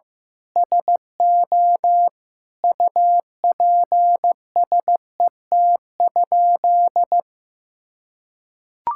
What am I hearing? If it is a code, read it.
Morse code, heard as a keyed tone, about 15 wpm, 700 Hz. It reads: SO UPSET?
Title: Sentences from Top 1000 Words - Review Part 5 - 15wpm